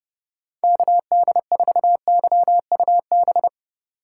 Morse code, heard as a keyed tone, about 36 wpm, 700 Hz. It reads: KD4YUB
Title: Speed Racing - USA 2x3 Call Signs 30wpm